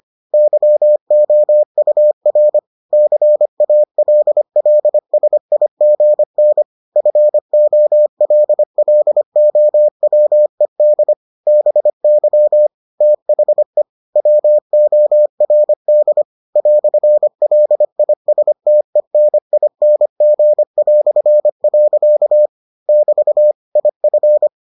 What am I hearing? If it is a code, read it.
Morse code, heard as a keyed tone, about 25 wpm, 600 Hz. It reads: YOUR CALLSIGN FOLLOWED BY THE WORD "LISTENING". <BT> IF